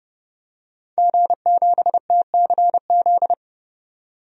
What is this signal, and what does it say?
Morse code, heard as a keyed tone, about 32 wpm, 700 Hz. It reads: G7TCZ
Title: DXCC-100 Call Signs - 5 Characters - 30wpm